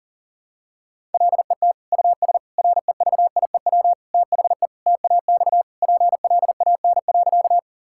Morse code, heard as a keyed tone, about 40 wpm, 700 Hz. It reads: LET US REVIEW THE TAX PLAN.
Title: Sentences from Top 1000 Words - Review Part 6 - 40wpm